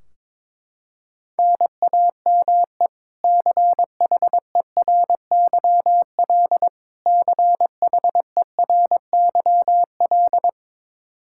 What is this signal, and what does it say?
Morse code, heard as a keyed tone, about 22 wpm, 700 Hz. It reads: NAME CHERYL CHERYL
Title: K1USN SST QSO Practice 22wpm